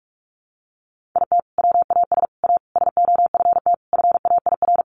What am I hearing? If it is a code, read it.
Morse code, heard as a keyed tone, about 44 wpm, 700 Hz. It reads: IT WAS A SOFT FAIR
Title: Sentences from Top 1500 Words 45wpm